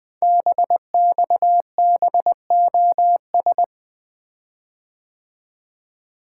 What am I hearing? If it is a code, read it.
Morse code, heard as a keyed tone, about 20 wpm, 700 Hz. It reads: BXBOS
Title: Code Groups - 5 Letters - 20wpm